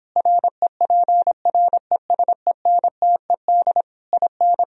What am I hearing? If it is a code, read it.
Morse code, heard as a keyed tone, about 26 wpm, 700 Hz. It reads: REPRESENTED IN